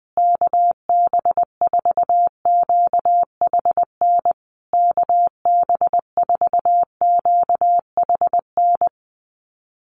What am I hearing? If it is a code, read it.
Morse code, heard as a keyed tone, about 20 wpm, 700 Hz. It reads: KB4QHN KB4QHN
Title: USA Call Signs Repeated 3x 20wpm